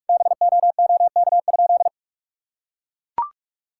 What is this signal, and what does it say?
Morse code, heard as a keyed tone, about 41 wpm, 700 Hz. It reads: BOOK?
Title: Sentences from Top 1000 Words - Review Part 5 - 45wpm